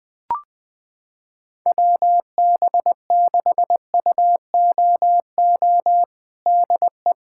Morse code, heard as a keyed tone, about 21 wpm, 700 Hz. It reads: WB6UOO DE
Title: USA Callsign DE Callsign 20wpm